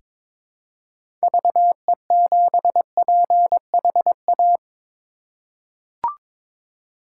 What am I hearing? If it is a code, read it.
Morse code, heard as a keyed tone, about 22 wpm, 700 Hz. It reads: VE7PHA
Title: K1USN SST QSO Practice - 22wpm